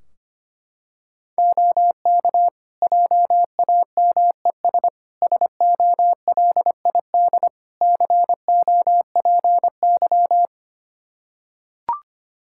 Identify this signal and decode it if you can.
Morse code, heard as a keyed tone, about 25 wpm, 700 Hz. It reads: OK JAMES SOLID COPY
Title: Practice QSO - Exchange 3 - More Info - 25wpm